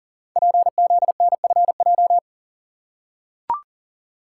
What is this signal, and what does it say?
Morse code, heard as a keyed tone, about 38 wpm, 700 Hz. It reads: PZNFJ
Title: Code Groups - 5 Letters - 40wpm